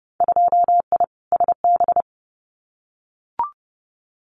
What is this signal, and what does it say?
Morse code, heard as a keyed tone, about 30 wpm, 700 Hz. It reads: 2I SB
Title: Winter Field Day Exchanges Encore 30wpm